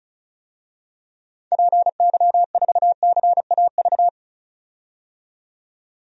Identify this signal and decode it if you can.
Morse code, heard as a keyed tone, about 36 wpm, 700 Hz. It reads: PY4CAV
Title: DXCC-100 Call Signs - 6 Characters - 35wpm